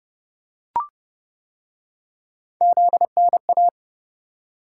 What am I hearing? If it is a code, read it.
Morse code, heard as a keyed tone, about 26 wpm, 700 Hz. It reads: ZNA